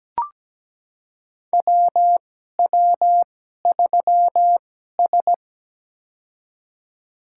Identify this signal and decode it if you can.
Morse code, heard as a keyed tone, about 17 wpm, spaced as for 12 wpm, 700 Hz. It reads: WW3S
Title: USA Call Signs - 4 Characters - 2x Character Spacing - 17wpm